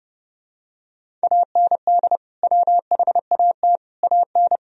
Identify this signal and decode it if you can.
Morse code, heard as a keyed tone, about 30 wpm, 700 Hz. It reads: AND WHAT AN
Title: Sentences from Top 1000 Words - Review Part 1 - 30wpm